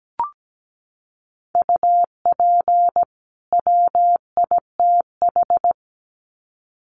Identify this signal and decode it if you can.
Morse code, heard as a keyed tone, about 17 wpm, 700 Hz. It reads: UP WITH
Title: Sets of 2 Words 17wpm